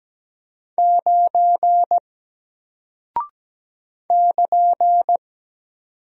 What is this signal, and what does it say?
Morse code, heard as a keyed tone, about 17 wpm, 700 Hz. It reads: 9 <KN>
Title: Single Character (Letter:Number:Punctuation:Prosign) Alt 17wpm